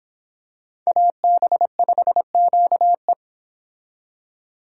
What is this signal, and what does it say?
Morse code, heard as a keyed tone, about 25 wpm, 700 Hz. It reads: AB5QE